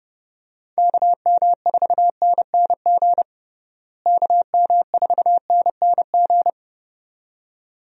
Morse code, heard as a keyed tone, about 31 wpm, 700 Hz. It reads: KM4NNG KM4NNG
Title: USA Call Signs - 6 Characters - Repeated 2x - 30wpm